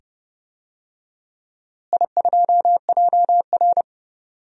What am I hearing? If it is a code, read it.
Morse code, heard as a keyed tone, about 30 wpm, 700 Hz. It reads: I2JR
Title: DXCC-100 Call Signs - 4 Characters - Repeated 2x - 30wpm